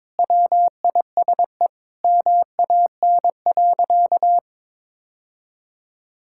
Morse code, heard as a keyed tone, about 22 wpm, 700 Hz. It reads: WISE MAN.
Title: Sentences from Top 1600 Words 22wpm